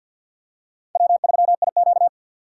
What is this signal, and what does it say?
Morse code, heard as a keyed tone, about 53 wpm, 700 Hz. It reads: W3IQ
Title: USA Call Signs - 4 Characters - 2x Character Spacing - 50wpm